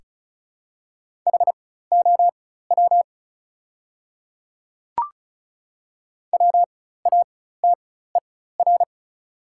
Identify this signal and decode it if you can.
Morse code, heard as a keyed tone, about 36 wpm, spaced as for 16 wpm, 700 Hz. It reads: HOW WATER
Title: 4x Character Spacing - Top 100 Words - 35wpm